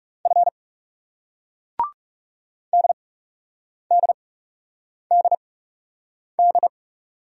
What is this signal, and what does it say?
Morse code, heard as a keyed tone, about 38 wpm, 700 Hz. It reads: F D D D D